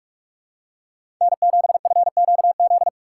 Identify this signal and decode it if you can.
Morse code, heard as a keyed tone, about 46 wpm, 700 Hz. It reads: N7FQZ
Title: USA Call Signs - 5 Characters - 2x Character Spacing - 45wpm